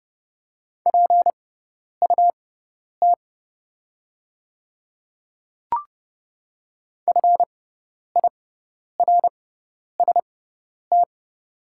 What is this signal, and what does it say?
Morse code, heard as a keyed tone, about 30 wpm, spaced as for 10 wpm, 700 Hz. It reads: PUT FIRST
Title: 6x Character Spacing - Top 500 Words - 30wpm